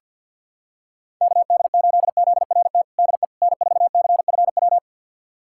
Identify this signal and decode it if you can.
Morse code, heard as a keyed tone, about 46 wpm, 700 Hz. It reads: KD8ZRT DE N4CFW
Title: USA Callsign DE Callsign 50wpm